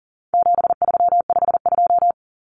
Morse code, heard as a keyed tone, about 40 wpm, 700 Hz. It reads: 7352
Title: Numbers - 4-Digit 40wpm